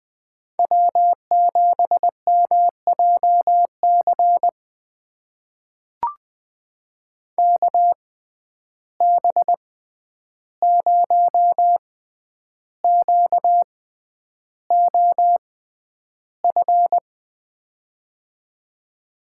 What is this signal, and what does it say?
Morse code, heard as a keyed tone, about 20 wpm, spaced as for 10 wpm, 700 Hz. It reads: W7MJC KB0QOF
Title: USA Call Signs - 6x Character Spacing - 20wpm